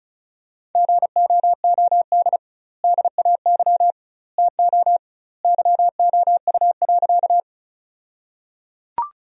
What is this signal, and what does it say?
Morse code, heard as a keyed tone, about 36 wpm, 700 Hz. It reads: GOOD DAY TO YOU.
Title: 2x Word-Spacing - Sentences from Top 200 Words - 35wpm